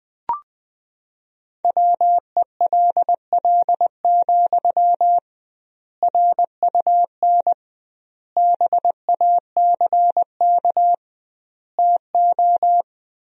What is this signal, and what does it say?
Morse code, heard as a keyed tone, about 20 wpm, 700 Hz. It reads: WELL, RUN BACK TO
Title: 2x Word-Spacing - Sentences from Top 300 Words - 20wpm